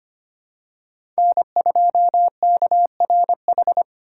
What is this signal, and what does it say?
Morse code, heard as a keyed tone, about 25 wpm, 700 Hz. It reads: N2KRH
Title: USA Call Signs - 5 Characters - 2x Character Spacing - 25wpm